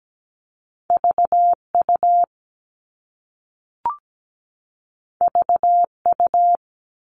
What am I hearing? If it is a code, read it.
Morse code, heard as a keyed tone, about 17 wpm, 700 Hz. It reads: VU VU
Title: All Cut Numbers - 2-Digit 17wpm